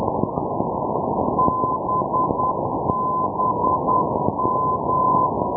Morse code, heard as a keyed tone, about 10 wpm, 1020 Hz, with noise. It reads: BBM
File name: europe / pol / pol-480 ndb zielona gora - babimost bbm-390-rb-d-bastei-2012